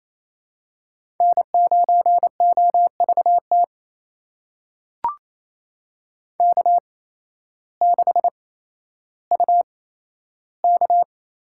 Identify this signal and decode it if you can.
Morse code, heard as a keyed tone, about 28 wpm, spaced as for 13 wpm, 700 Hz. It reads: N9OVT K6UK
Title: USA Call Signs - 5 Characters - 8x Character Spacing - 28wpm